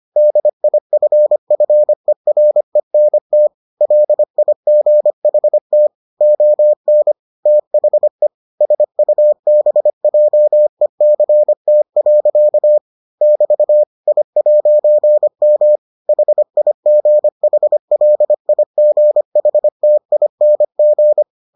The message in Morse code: DIFFERENT LIGHT ON THE SUBJECT. <BT> I'M HIGHLIGHTING

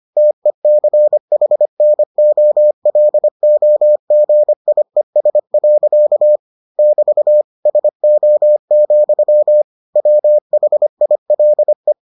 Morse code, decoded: TECHNOLOGIES. <BT> SO, WHILE

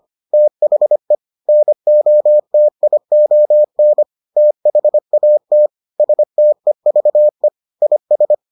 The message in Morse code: THE NOTION THAT STEVE IS

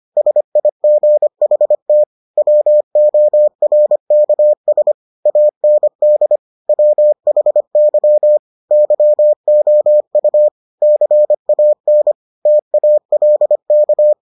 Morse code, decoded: SIGHT WORKS AND WHY YOU CAN TALK

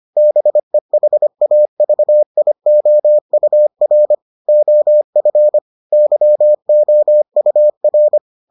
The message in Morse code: BEHAVIOUR OF YOUR

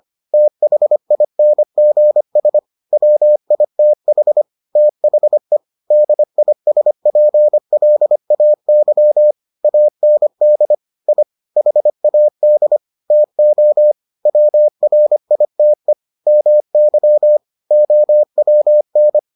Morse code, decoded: THINGS WITH THE DISPLAY AND I HAD TO WRITE MY OWN

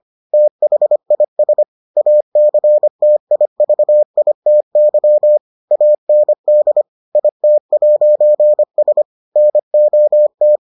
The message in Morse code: THIS ACTIVITY AND IT'S NOT

